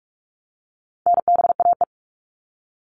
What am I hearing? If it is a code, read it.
Morse code, heard as a keyed tone, about 44 wpm, 700 Hz. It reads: NBAE
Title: All Cut Numbers - 4-Digit 45wpm